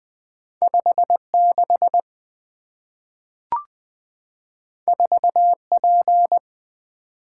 Morse code, heard as a keyed tone, about 20 wpm, 700 Hz. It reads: 56 4P
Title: Two Letter-Number Combinations - 20wpm